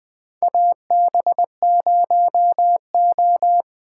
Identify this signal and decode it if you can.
Morse code, heard as a keyed tone, about 20 wpm, 700 Hz. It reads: AB0O
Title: 2023 POTA Activator Call Signs - 20wpm